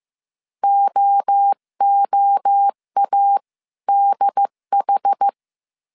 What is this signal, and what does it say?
Morse code, heard as a keyed tone, about 15 wpm, spaced as for 13 wpm, 800 Hz. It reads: OOADH
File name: CW / DET / 1 3 1500C S A 14wpm 2447